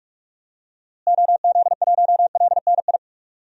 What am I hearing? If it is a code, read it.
Morse code, heard as a keyed tone, about 41 wpm, 700 Hz. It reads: OZ1LNI
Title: DXCC-100 Call Signs - 6 Characters - Repeated 3x - 45wpm